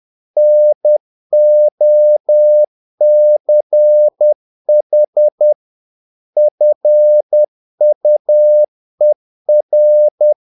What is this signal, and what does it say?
Morse code, heard as a keyed tone, about 10 wpm, 600 Hz. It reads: NOCH FUER
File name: dk5ke / Humor-50BpM